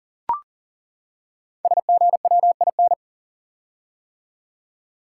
Morse code, heard as a keyed tone, about 40 wpm, 690 Hz. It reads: SGWIN
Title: Code Groups - 5 Letters - 40wpm